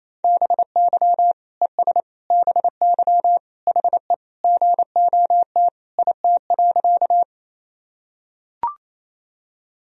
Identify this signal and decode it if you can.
Morse code, heard as a keyed tone, about 28 wpm, 700 Hz. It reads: BY ES BY HE GOT IT.
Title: Sentences from Top 300 Words and Abbreviations 28wpm